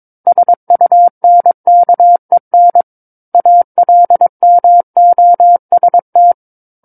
Morse code, heard as a keyed tone, about 22 wpm, 700 Hz. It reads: SUNKEN ALMOST
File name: Book One / 13-Chapter 13 - 22 WPM